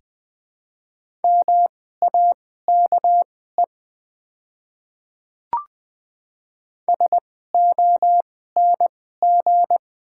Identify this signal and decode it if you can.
Morse code, heard as a keyed tone, about 20 wpm, spaced as for 14 wpm, 690 Hz. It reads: MAKE SONG
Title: 2x Character Spacing - Top 500 Words - 20wpm